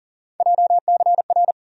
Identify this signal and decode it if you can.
Morse code, heard as a keyed tone, about 40 wpm, 700 Hz. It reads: JCR
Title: USA Call Sign Postfixes - 3 Letters 40wpm